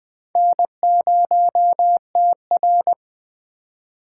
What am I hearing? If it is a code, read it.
Morse code, heard as a keyed tone, about 21 wpm, 700 Hz. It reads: N0TR